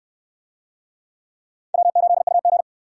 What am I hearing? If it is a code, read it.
Morse code, heard as a keyed tone, about 59 wpm, 700 Hz. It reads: W8UG